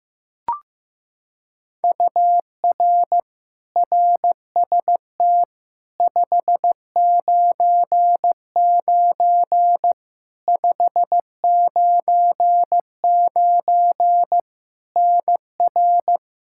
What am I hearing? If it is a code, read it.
Morse code, heard as a keyed tone, about 15 wpm, 700 Hz. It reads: UR RST 599 599 NR
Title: Practice QSO - Exchange 1 - Beginning - 15wpm